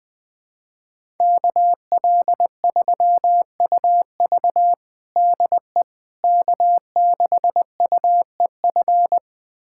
Morse code, heard as a keyed tone, about 20 wpm, 700 Hz. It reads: KL3UV DE K6UEF
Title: USA Callsign DE Callsign - 5 Characters - 20wpm